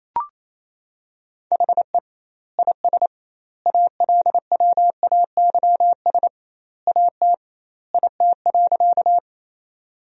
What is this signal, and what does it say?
Morse code, heard as a keyed tone, about 28 wpm, 700 Hz. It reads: HE IS ALWAYS AT IT.
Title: 2x Word-Spacing - Sentences from Top 300 Words - 28wpm